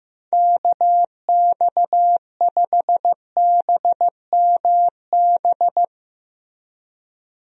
Morse code, heard as a keyed tone, about 15 wpm, 700 Hz. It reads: KX5BMB
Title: USA Call Signs - 6 Characters - Repeated 2x - 15wpm